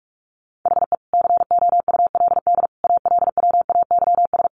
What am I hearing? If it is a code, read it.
Morse code, heard as a keyed tone, about 44 wpm, 700 Hz. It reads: HE COULD ALWAYS
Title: Sentences from Top 1400 Words 45wpm